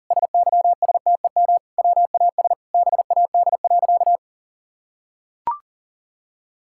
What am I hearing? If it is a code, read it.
Morse code, heard as a keyed tone, about 40 wpm, 700 Hz. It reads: SYSTEM WAS BAD.
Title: Sentences from Top 1000 Words - Review Part 5 - 40wpm